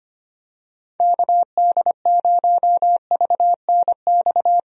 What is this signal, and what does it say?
Morse code, heard as a keyed tone, about 25 wpm, 700 Hz. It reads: KD0VNX
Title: USA Call Signs - 6 Characters - Repeated 2x - 25wpm